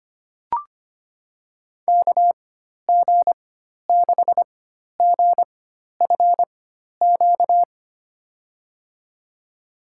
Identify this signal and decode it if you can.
Morse code, heard as a keyed tone, about 25 wpm, spaced as for 12 wpm, 700 Hz. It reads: KG6GFQ
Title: USA Call Signs - 6 Characters - 4x Character Spacing - 25wpm